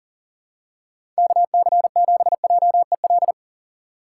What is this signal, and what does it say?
Morse code, heard as a keyed tone, about 40 wpm, 700 Hz. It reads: KC7JEL
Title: USA Call Signs - 6 Characters - 3x Character Spacing - 40wpm